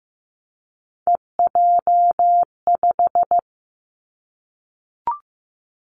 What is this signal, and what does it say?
Morse code, heard as a keyed tone, about 15 wpm, 700 Hz. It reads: EJ5
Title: Three Challenging Letter-Number Combinations 15wpm